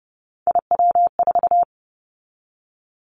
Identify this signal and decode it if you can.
Morse code, heard as a keyed tone, about 30 wpm, 700 Hz. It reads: IW4
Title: Callsign Prefixes for Top 25 DXCC - 30wpm